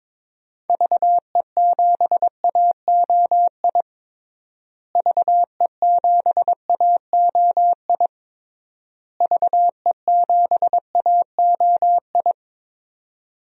Morse code, heard as a keyed tone, about 22 wpm, 700 Hz. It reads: VE7AOI VE7AOI VE7AOI
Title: DXCC-100 Call Signs - 6 Characters - Repeated 3x - 22wpm